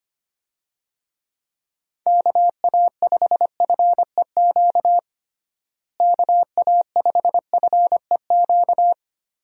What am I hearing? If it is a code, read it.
Morse code, heard as a keyed tone, about 25 wpm, 700 Hz. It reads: KA5FEQ KA5FEQ
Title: USA Call Signs - 6 Characters - Repeated 3x - 25wpm